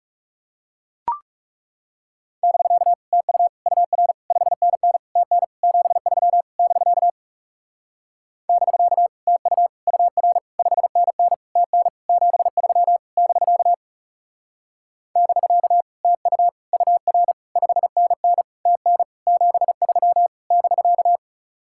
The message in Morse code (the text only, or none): <BK> TU UR 5NN TN 73 <BK> <BK> TU UR 5NN TN 73 <BK> <BK> TU UR 5NN TN 73 <BK>